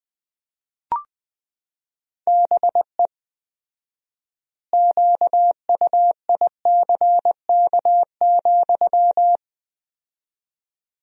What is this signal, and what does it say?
Morse code, heard as a keyed tone, about 21 wpm, 700 Hz. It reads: BE QUICK,
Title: 4x Word-Spacing - Sentences from Top 500 Words - 20wpm